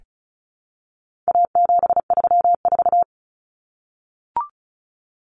Morse code, heard as a keyed tone, about 34 wpm, 700 Hz. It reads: A734
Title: Common Cut Numbers - 4-Digit 35wpm